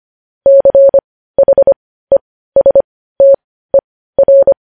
Morse code, heard as a keyed tone, about 25 wpm, spaced as for 15 wpm, 550 Hz. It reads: CHESTER
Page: news-headlines-in-morse-code-at-15-wpm